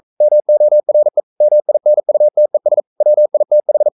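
Morse code, heard as a keyed tone, about 42 wpm, 600 Hz. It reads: MORE MINUTES WITH
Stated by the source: Morse code sending of an English novel at 42 wpm.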